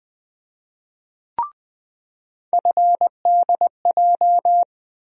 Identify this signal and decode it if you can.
Morse code, heard as a keyed tone, about 20 wpm, 700 Hz. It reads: FDJ
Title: USA Call Sign Postfixes - 3 Letters 20wpm